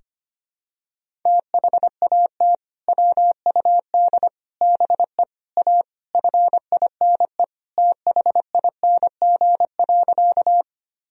Morse code, heard as a keyed tone, about 25 wpm, 700 Hz. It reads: THAT WUD BE A FINE THING.
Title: Sentences from Top 500 Words and Abbreviations 25wpm